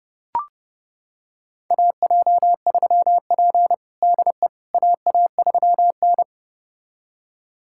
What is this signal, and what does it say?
Morse code, heard as a keyed tone, about 32 wpm, 700 Hz. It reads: AJ3P DE AA3N
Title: USA Callsign DE Callsign - 4 Characters - 30wpm